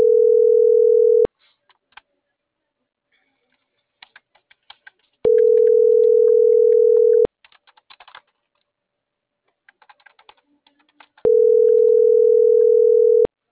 Ringback tone (North America), 440 + 480 Hz.